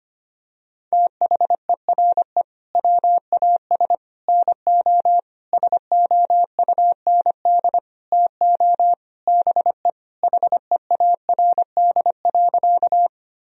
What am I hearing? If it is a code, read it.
Morse code, heard as a keyed tone, about 25 wpm, 700 Hz. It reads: THERE WAS NO SOUND TO BE HEARD.
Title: Sentences from Top 1000 Words - Review Part 1 - 25wpm